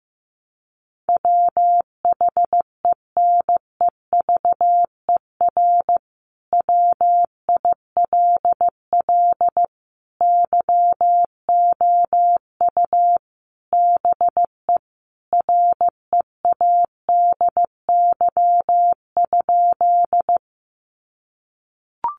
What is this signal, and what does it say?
Morse code, heard as a keyed tone, about 15 wpm, 700 Hz. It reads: WHENEVER WILL YOU BE READY?
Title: Sentences from Top 1200 Words 15wpm